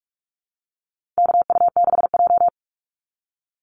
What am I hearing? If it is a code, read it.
Morse code, heard as a keyed tone, about 44 wpm, 700 Hz. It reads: KU6J
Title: NA SOTA Activator Callsigns 45wpm